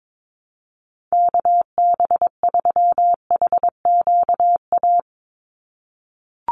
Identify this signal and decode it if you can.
Morse code, heard as a keyed tone, about 22 wpm, 700 Hz. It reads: KB3HQA
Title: USA Call Signs 22wpm